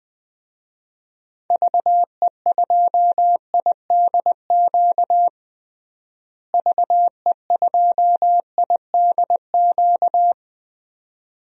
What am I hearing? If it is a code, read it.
Morse code, heard as a keyed tone, about 20 wpm, 700 Hz. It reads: VE2IDQ VE2IDQ
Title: DXCC-100 Call Signs - 6 Characters - Repeated 2x - 20wpm